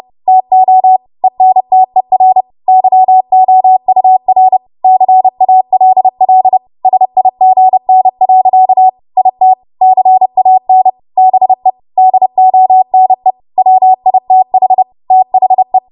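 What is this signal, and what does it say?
Morse code, heard as a keyed tone, about 30 wpm, 750 Hz. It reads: TO ENTER YOUR CALL SIGN. IT CAN BE DONE WITH THE
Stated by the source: Morse practice transmission at 30 WPM.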